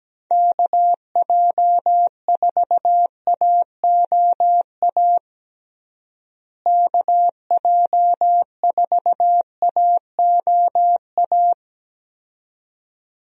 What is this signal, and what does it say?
Morse code, heard as a keyed tone, about 17 wpm, 700 Hz. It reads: KJ4AOA KJ4AOA
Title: USA Call Signs - 6 Characters - Repeated 2x - 17wpm